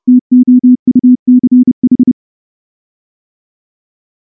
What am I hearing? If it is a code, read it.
Morse code, heard as a keyed tone, about 29 wpm, 250 Hz. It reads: TOUCH
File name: morse / morse-training-30WPM